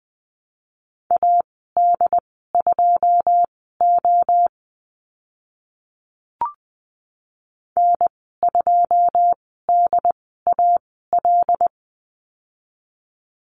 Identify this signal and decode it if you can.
Morse code, heard as a keyed tone, about 20 wpm, spaced as for 14 wpm, 700 Hz. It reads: AD2O N2DAL
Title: 082 Lesson 14g - Number 2 - Call Signs 20wpm